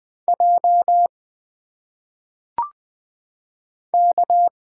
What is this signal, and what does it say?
Morse code, heard as a keyed tone, about 20 wpm, 700 Hz. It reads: J K
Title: Single Letters - Phonetic Pronunciation 20wpm